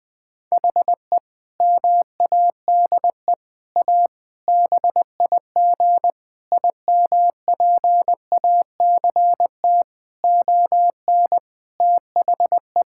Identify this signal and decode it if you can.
Morse code, heard as a keyed tone, about 20 wpm, 700 Hz. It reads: HE MADE A BIG IMPACT ON THE
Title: Sentences from Top 1000 Words - Review Part 6 - 20wpm